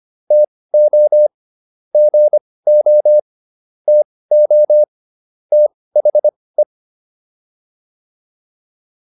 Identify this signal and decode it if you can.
Morse code, heard as a keyed tone, about 25 wpm, spaced as for 18 wpm, 600 Hz. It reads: TO GO TO THE